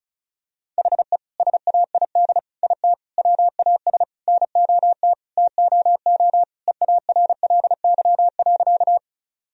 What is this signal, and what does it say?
Morse code, heard as a keyed tone, about 34 wpm, 700 Hz. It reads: HE SAID IT WAS NOT TOO EARLY.
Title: Sentences from Top 1000 Words - Review Part 2 - 35wpm